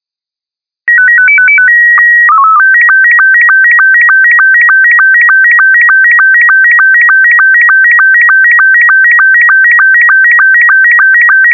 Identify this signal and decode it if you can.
SSTV (slow-scan television), Robot 72.